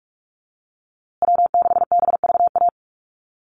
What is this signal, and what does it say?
Morse code, heard as a keyed tone, about 44 wpm, 700 Hz. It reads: W6BVA
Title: USA Call Signs Repeated 2x 45wpm